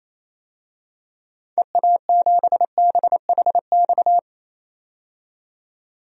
Morse code, heard as a keyed tone, about 27 wpm, 700 Hz. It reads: EA7BHX
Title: DXCC-100 Call Signs - 6 Characters - 28wpm